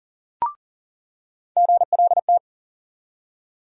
Morse code, heard as a keyed tone, about 40 wpm, 700 Hz. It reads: GLT
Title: USA Call Sign Postfixes - 3 Letters 40wpm